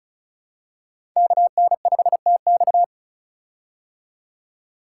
Morse code, heard as a keyed tone, about 34 wpm, 700 Hz. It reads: KN5TX
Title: K1USN SST QSO Practice - 35wpm